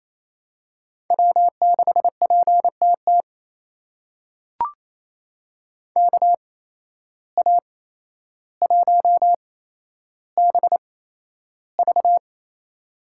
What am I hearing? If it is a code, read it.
Morse code, heard as a keyed tone, about 28 wpm, spaced as for 12 wpm, 700 Hz. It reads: W6PTT KA1BV
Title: USA Call Signs - 5 Characters - 8x Character Spacing - 28wpm